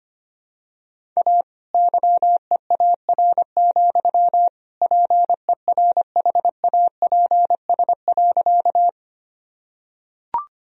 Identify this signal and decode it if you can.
Morse code, heard as a keyed tone, about 25 wpm, 700 Hz. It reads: A YEAR, PERHAPS.
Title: Sentences from Top 1000 Words - Review Part 1 - 25wpm